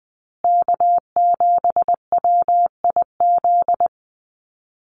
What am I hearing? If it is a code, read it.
Morse code, heard as a keyed tone, about 20 wpm, 700 Hz. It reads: K7WIZ
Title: USA Call Signs Repeated 2x 20wpm